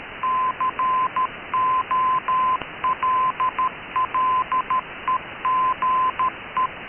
Morse code, heard as a keyed tone, about 13 wpm, 1030 Hz. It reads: COLLEGE
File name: KKS-1 cw 21jul2014 1425z